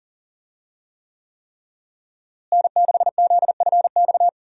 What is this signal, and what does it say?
Morse code, heard as a keyed tone, about 40 wpm, 700 Hz. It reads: N6ZFX
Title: USA Call Signs - 5 Characters - Repeated 3x - 40wpm